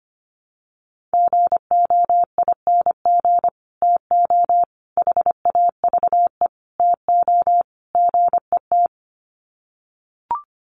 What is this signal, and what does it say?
Morse code, heard as a keyed tone, about 25 wpm, 700 Hz. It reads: GOING TO HAVE TO GET
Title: Sets of 5 Words Encore2 25wpm